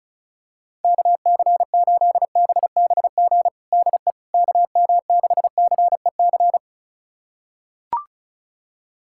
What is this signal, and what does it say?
Morse code, heard as a keyed tone, about 36 wpm, 690 Hz. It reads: KC8BBG DE KM6CEC
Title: USA Callsign DE Callsign 35wpm